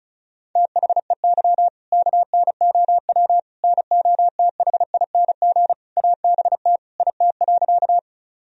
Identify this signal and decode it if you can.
Morse code, heard as a keyed tone, about 34 wpm, 700 Hz. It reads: THEY KNOW NOTHING ABT IT.
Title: Sentences from Top 500 Words and Abbreviations 35wpm